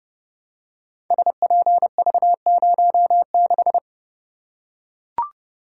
Morse code, heard as a keyed tone, about 29 wpm, 700 Hz. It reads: SPV06
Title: Code Groups - 5 Letter:Numbers - 30wpm